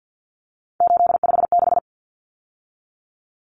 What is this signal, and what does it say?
Morse code, heard as a keyed tone, about 50 wpm, 700 Hz. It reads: 856
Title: Numbers - 3-Digit 50wpm